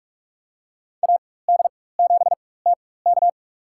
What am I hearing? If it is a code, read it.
Morse code, heard as a keyed tone, about 46 wpm, spaced as for 21 wpm, 700 Hz. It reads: AD7TK
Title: USA Call Signs - 4x Character Spacing - 45wpm